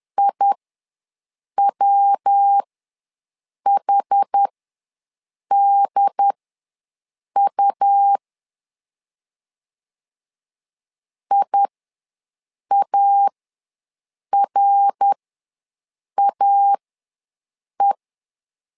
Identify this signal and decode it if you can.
Morse code, heard as a keyed tone, about 11 wpm, spaced as for 6 wpm, 800 Hz. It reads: IWHDU IARAE
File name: LEZIONI / 3 MWRO / 3 MWR ws 10wpm 2151